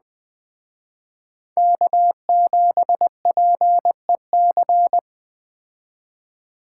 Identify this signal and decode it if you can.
Morse code, heard as a keyed tone, about 20 wpm, 700 Hz. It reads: K7PEC
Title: Speed Racing - USA 1x3 Call Signs 17wpm